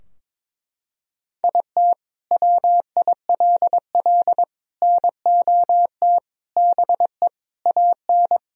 Morse code, heard as a keyed tone, about 22 wpm, 700 Hz. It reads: IT WILL NOT BE AN
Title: Sentences from Top 1000 Words - Review Part 2 - 22wpm